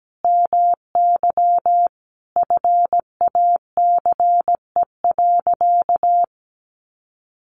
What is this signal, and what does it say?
Morse code, heard as a keyed tone, about 17 wpm, 700 Hz. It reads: MY FACE.
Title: Sentences from Top 1100 Words 17wpm